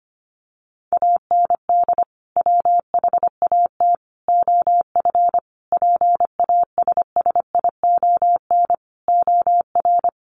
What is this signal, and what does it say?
Morse code, heard as a keyed tone, about 25 wpm, 700 Hz. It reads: AND WHAT OF PASSION OR
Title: Sentences from Top 1400 Words 25wpm